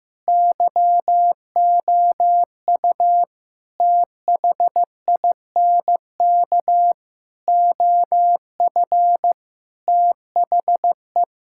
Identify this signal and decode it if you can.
Morse code, heard as a keyed tone, about 16 wpm, 700 Hz. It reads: YOU THINK OF THE
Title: Sentences from Top 1000 Words - Review Part 5 - 15wpm